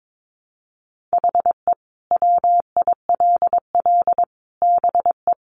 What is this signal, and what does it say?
Morse code, heard as a keyed tone, about 22 wpm, 700 Hz. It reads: HE WILL BE
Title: Sentences from Top 600 Words 22wpm